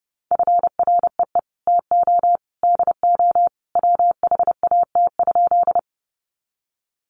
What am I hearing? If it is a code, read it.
Morse code, heard as a keyed tone, about 30 wpm, 700 Hz. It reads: FREE TO DO WHAT?
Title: Sentences from Top 500 Words 30wpm